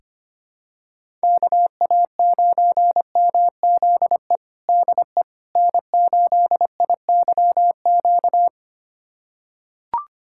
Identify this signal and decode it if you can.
Morse code, heard as a keyed tone, about 25 wpm, 700 Hz. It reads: KA9MZE DE N8IYQ
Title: USA Callsign DE Callsign 25wpm